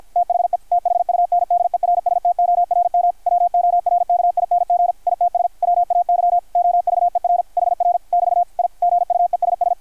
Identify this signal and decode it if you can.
Morse code, heard as a keyed tone, about 53 wpm, 660 Hz. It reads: THE THUNDERSTORM WORKING ITS WAY OVER HR <BT> I GUESS
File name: archived / CRSnet